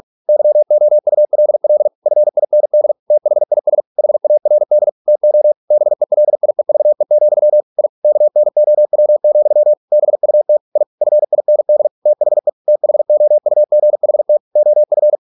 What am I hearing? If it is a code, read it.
Morse code, heard as a keyed tone, about 46 wpm, 600 Hz. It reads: YOULL FIND THIS HARD TO BELIEVE, I KNOW, BUT I FIND THE THOUGHT OF